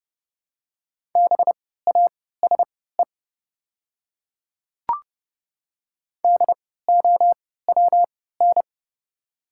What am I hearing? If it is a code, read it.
Morse code, heard as a keyed tone, about 30 wpm, spaced as for 17 wpm, 700 Hz. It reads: BASE DOWN
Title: 3x Character Spacing - Top 300 Words - 30wpm